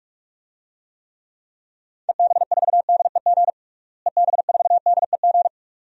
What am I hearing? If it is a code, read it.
Morse code, heard as a keyed tone, about 42 wpm, 700 Hz. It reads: EB4DEG EB4DEG
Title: DXCC-100 Call Signs - 6 Characters - Repeated 2x - 45wpm